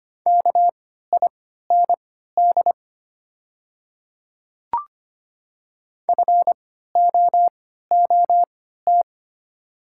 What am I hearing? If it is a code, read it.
Morse code, heard as a keyed tone, about 25 wpm, spaced as for 14 wpm, 700 Hz. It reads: KIND FOOT
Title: 3x Character Spacing - Top 500 Words - 25wpm